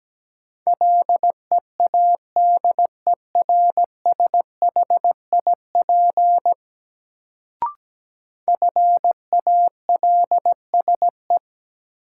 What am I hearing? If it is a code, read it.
Morse code, heard as a keyed tone, about 17 wpm, 700 Hz. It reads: LEADERSHIP FALSE
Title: Sentences from Top 1600 Words - New Word Drill - 17wpm